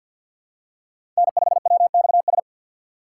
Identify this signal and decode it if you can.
Morse code, heard as a keyed tone, about 51 wpm, 700 Hz. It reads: N5WXS
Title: USA Call Signs - 5 Characters - 2x Character Spacing - 50wpm